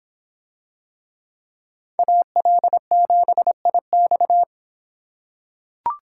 Morse code, heard as a keyed tone, about 25 wpm, 700 Hz. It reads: AL7IX